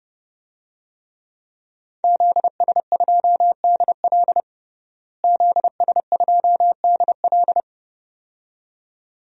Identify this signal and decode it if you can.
Morse code, heard as a keyed tone, about 30 wpm, 700 Hz. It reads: ZS2DL ZS2DL
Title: DXCC-100 Call Signs - 5 Characters - Repeated 2x - 30wpm